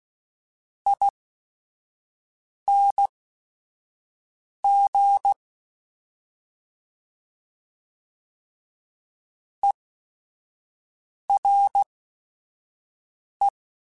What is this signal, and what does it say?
Morse code, heard as a keyed tone, about 16 wpm, spaced as for 5 wpm, 800 Hz. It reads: ING ERE